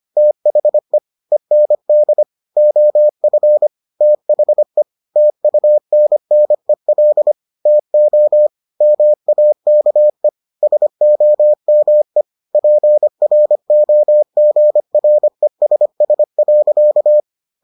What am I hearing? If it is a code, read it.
Morse code, heard as a keyed tone, about 25 wpm, 600 Hz. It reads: THE END OF THE TUNNEL TO MAKE SOME PROGRESS.